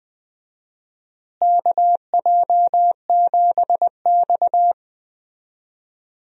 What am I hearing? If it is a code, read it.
Morse code, heard as a keyed tone, about 21 wpm, 700 Hz. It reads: KJ7X
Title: USA Call Signs - 4 Characters - Repeated 3x - 20wpm